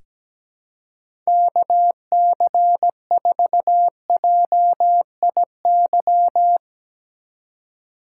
Morse code, heard as a keyed tone, about 17 wpm, 700 Hz. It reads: KC4JIY
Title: USA Call Signs - 4x Character Spacing - 17wpm